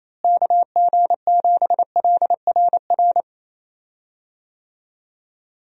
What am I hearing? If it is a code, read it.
Morse code, heard as a keyed tone, about 28 wpm, 700 Hz. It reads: KG7LRR
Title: USA Call Signs - 6 Characters - Repeated 3x - 28wpm